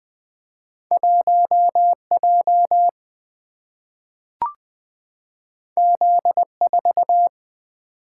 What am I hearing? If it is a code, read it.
Morse code, heard as a keyed tone, about 20 wpm, 700 Hz. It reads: 1J Z4
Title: Single-Letter Single-Number 20wpm